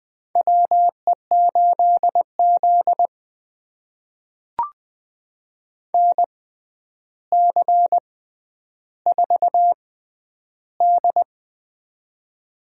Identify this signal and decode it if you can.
Morse code, heard as a keyed tone, about 20 wpm, spaced as for 10 wpm, 700 Hz. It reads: WE8Z NC4D
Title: USA Call Signs - 6x Character Spacing - 20wpm